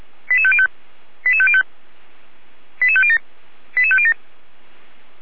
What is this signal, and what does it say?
Five-tone selective call (ZVEI), heard as quick runs of five tones, tones of about 70 ms. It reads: 88505 88505 88507 88507
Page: SDR/ZVEI